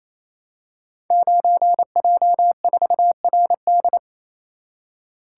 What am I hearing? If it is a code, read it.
Morse code, heard as a keyed tone, about 27 wpm, 700 Hz. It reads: 9J4RD
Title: Code Groups - 5 Letter:Numbers - 28wpm